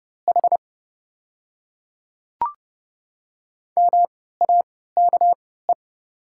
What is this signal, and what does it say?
Morse code, heard as a keyed tone, about 30 wpm, spaced as for 17 wpm, 690 Hz. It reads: H MAKE
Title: 3x Character Spacing - Top 100 Words - 30wpm